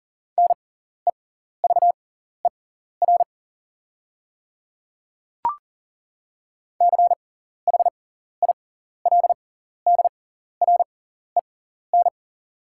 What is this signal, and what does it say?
Morse code, heard as a keyed tone, about 40 wpm, spaced as for 14 wpm, 700 Hz. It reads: NEVER CHILDREN
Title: 6x Character Spacing - Top 500 Words - 40wpm